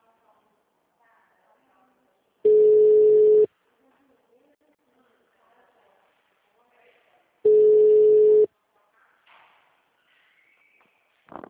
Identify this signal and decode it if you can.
Ringback tone (Europe, 425 Hz), with speech.
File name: out- -2015- - - .54442